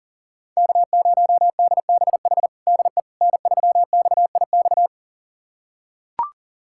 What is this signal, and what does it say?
Morse code, heard as a keyed tone, about 37 wpm, 690 Hz. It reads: K0DBH DE N3XIX
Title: USA Callsign DE Callsign - 5 Characters - 40wpm